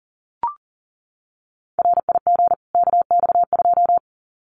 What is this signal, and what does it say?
Morse code, heard as a keyed tone, about 40 wpm, 700 Hz. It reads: RIG KX2
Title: Common QSO Phrases 40wpm